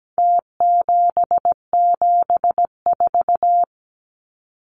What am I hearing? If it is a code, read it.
Morse code, heard as a keyed tone, about 17 wpm, 700 Hz. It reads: T774
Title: Numbers - Contest Style - 4-Digit 17wpm